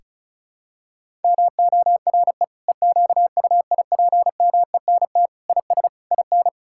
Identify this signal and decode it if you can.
Morse code, heard as a keyed tone, about 36 wpm, 700 Hz. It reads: MORE EQUIPMENT IS IN
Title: Sentences from Top 1000 Words - Review Part 5 - 35wpm